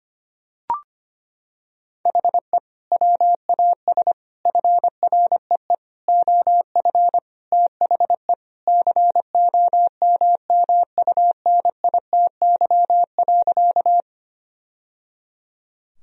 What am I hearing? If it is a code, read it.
Morse code, heard as a keyed tone, about 25 wpm, 700 Hz. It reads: HE WAS FREE OF THE COMMUNITY.
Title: Sentences from Top 1000 Words - Review Part 4 - 25wpm